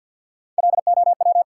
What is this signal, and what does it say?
Morse code, heard as a keyed tone, about 53 wpm, 700 Hz. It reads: LOW
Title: Sentences from Top 500 Words - New Word Drill - 50wpm